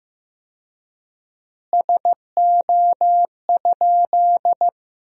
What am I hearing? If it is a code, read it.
Morse code, heard as a keyed tone, about 15 wpm, 700 Hz. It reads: SO?
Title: 4x Word-Spacing - Sentences from Top 200 Words - 15wpm